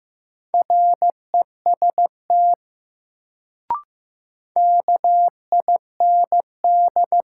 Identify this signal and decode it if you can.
Morse code, heard as a keyed tone, about 15 wpm, 700 Hz. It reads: REST KIND
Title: Sets of 2 Words - New Word Drill - 15wpm